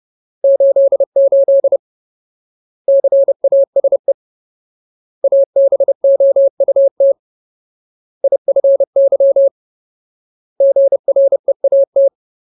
Morse code, heard as a keyed tone, about 30 wpm, 550 Hz. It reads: 88 CASE ABOUT IFY GREAT
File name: Ultimate Word List / 30WPM Ultimate